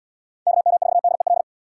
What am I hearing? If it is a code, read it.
Morse code, heard as a keyed tone, about 77 wpm, 700 Hz. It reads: KM4DEL